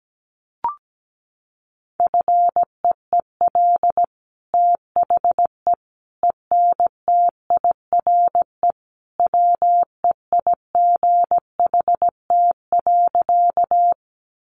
Morse code, heard as a keyed tone, about 17 wpm, 700 Hz. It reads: FEEL THE ENTIRE WEIGHT.
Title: Sentences from Top 800 Words 17wpm